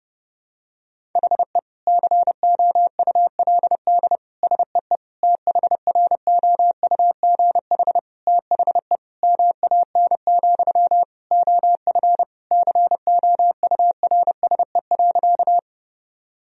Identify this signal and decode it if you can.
Morse code, heard as a keyed tone, about 30 wpm, 700 Hz. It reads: HE COULD SEE THROUGH THE MAN, OF COURSE.
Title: Sentences from Top 1000 Words - Review Part 1 - 30wpm